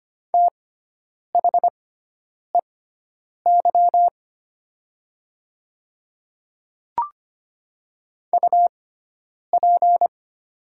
Morse code, heard as a keyed tone, about 25 wpm, spaced as for 9 wpm, 700 Hz. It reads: THEY UP
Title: 6x Character Spacing - Top 500 Words - 25wpm